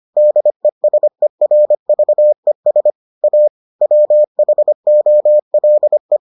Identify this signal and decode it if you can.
Morse code, heard as a keyed tone, about 25 wpm, 600 Hz. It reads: DESERVES A WHOLE